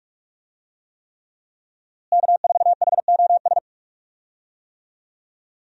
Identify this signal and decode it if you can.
Morse code, heard as a keyed tone, about 41 wpm, 700 Hz. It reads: K4HOS
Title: USA Call Signs - 5 Characters - Repeated 3x - 45wpm